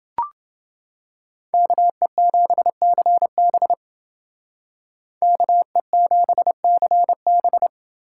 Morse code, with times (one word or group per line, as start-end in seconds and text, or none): 1.5-3.7 KE7CB
5.2-7.7 KE7CB